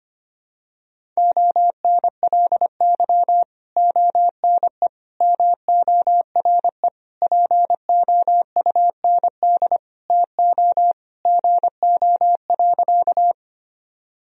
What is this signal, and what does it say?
Morse code, heard as a keyed tone, about 25 wpm, 700 Hz. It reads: ONLY ONE MORE POUND TO GO.
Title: Sentences from Top 1000 Words - Review Part 2 - 25wpm